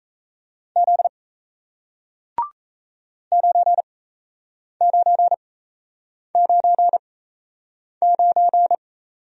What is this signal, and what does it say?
Morse code, heard as a keyed tone, about 38 wpm, spaced as for 8 wpm, 700 Hz. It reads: Z 9999